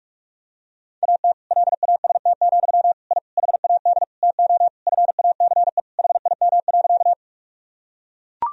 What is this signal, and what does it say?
Morse code, heard as a keyed tone, about 41 wpm, 700 Hz. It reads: AT LAST, I HAD TO FACE HIM.
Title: Sentences from Top 1000 Words - Review Part 2 - 45wpm